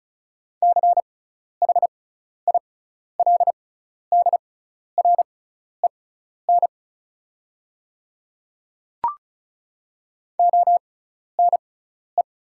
Morse code, heard as a keyed tone, about 36 wpm, spaced as for 12 wpm, 690 Hz. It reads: CHILDREN ONE
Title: 6x Character Spacing - Top 300 Words - 35wpm